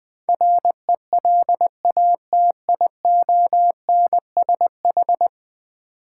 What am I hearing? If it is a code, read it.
Morse code, heard as a keyed tone, about 20 wpm, 700 Hz. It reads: RELATIONSH
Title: Build A Word - 5 Letters At A Time - 12-Letter Words - 20wpm